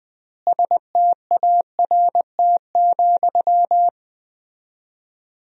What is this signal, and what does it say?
Morse code, heard as a keyed tone, about 21 wpm, 700 Hz. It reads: START,
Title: 4x Word-Spacing - Sentences from Top 300 Words - 20wpm